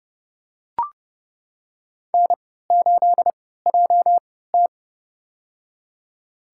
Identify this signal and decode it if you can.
Morse code, heard as a keyed tone, about 29 wpm, spaced as for 17 wpm, 700 Hz. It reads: N8JT
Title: USA Call Signs - 4 Characters - 3x Character Spacing - 30wpm